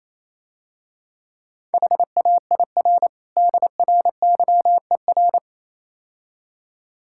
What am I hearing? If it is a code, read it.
Morse code, heard as a keyed tone, about 27 wpm, 700 Hz. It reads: HAIR DRYER
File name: Easy Words - Speed-Racing with Hints - 22wpm